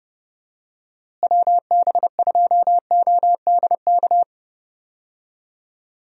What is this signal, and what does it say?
Morse code, heard as a keyed tone, about 30 wpm, 700 Hz. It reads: WB2ODK
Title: USA Call Signs - 6 Characters - Repeated 2x - 30wpm